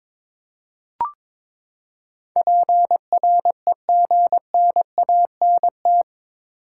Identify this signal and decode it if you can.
Morse code, heard as a keyed tone, about 22 wpm, 700 Hz. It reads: PREGNANT
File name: practice / 22wpm / Eight Letter Words - 22wpm